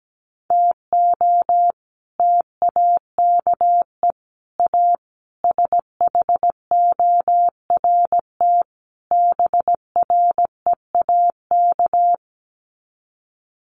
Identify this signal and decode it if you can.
Morse code, heard as a keyed tone, about 17 wpm, 700 Hz. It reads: TO TAKE A SHORT BREAK
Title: Sets of 5 Words Encore2 17wpm